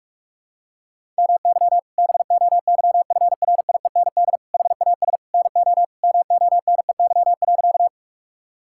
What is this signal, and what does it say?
Morse code, heard as a keyed tone, about 46 wpm, 700 Hz. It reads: MY BOYFRIEND HAS NO MONEY.
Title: Sentences from Top 1000 Words - Review Part 5 - 45wpm